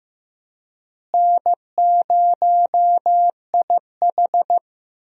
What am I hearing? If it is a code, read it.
Morse code, heard as a keyed tone, about 15 wpm, 700 Hz. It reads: N0IH
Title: USA Call Signs - 4 Characters - Repeated 2x - 15wpm